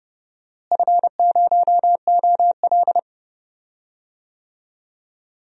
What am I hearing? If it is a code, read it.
Morse code, heard as a keyed tone, about 32 wpm, 700 Hz. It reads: F0OL
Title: DXCC-100 Call Signs - 4 Characters - Repeated 2x - 30wpm